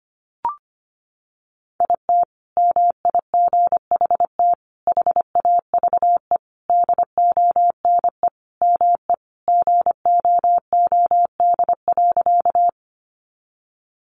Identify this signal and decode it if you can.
Morse code, heard as a keyed tone, about 25 wpm, 700 Hz. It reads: IT MIGHT HAVE DONE ME GOOD.
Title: Sentences from Top 500 Words 25wpm